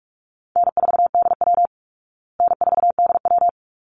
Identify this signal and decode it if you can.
Morse code, heard as a keyed tone, about 44 wpm, 700 Hz. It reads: N4DW N4DW
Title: Easy Call Signs Repeated 2x - 45wpm